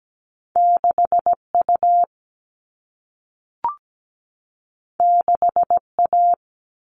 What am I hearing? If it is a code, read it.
Morse code, heard as a keyed tone, about 17 wpm, 700 Hz. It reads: 6U 6A
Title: All Cut Numbers - 2-Digit 17wpm